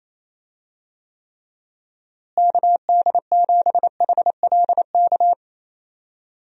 Morse code, heard as a keyed tone, about 28 wpm, 700 Hz. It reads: KD7HLK
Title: USA Call Signs - 6 Characters - Repeated 2x - 28wpm